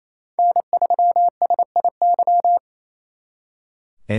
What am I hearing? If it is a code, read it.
Morse code, heard as a keyed tone, about 35 wpm, 700 Hz. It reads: N3SIY
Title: Speed Racing - USA 1x3 Call Signs 28wpm